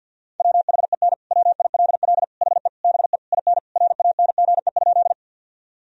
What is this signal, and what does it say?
Morse code, heard as a keyed tone, about 50 wpm, 700 Hz. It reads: WHEN WILL HE BE IN RANGE?
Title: Sentences from Top 1000 Words - Review Part 4 - 50wpm